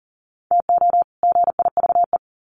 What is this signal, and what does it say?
Morse code, heard as a keyed tone, about 40 wpm, 700 Hz. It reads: TO GIVE
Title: Sets of 2 Words 40wpm